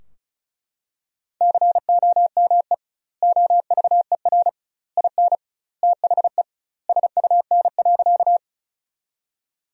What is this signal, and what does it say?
Morse code, heard as a keyed tone, about 36 wpm, 700 Hz. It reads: COME OVER IN THE SUN.
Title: 2x Word-Spacing - Sentences from Top 300 Words - 35wpm